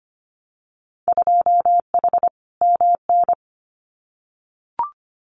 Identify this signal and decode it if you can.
Morse code, heard as a keyed tone, about 25 wpm, 700 Hz. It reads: 2H MN
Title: Winter Field Day Exchanges 25wpm